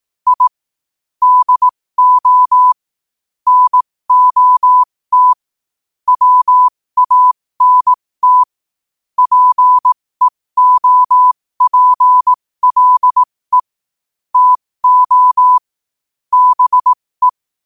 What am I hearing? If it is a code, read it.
Morse code, heard as a keyed tone, about 18 wpm, spaced as for 15 wpm, 1000 Hz. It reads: I DO NOT WANT PEOPLE TO BE
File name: qotdfiles / 15wpm